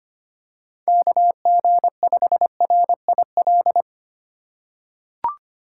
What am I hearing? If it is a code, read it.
Morse code, heard as a keyed tone, about 25 wpm, 700 Hz. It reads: KG5RIL
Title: USA Call Signs - 6 Characters - 2x Character Spacing - 25wpm